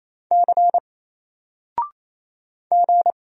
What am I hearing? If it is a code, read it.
Morse code, heard as a keyed tone, about 27 wpm, 700 Hz. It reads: C G
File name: Single-Letter-Number-28wpm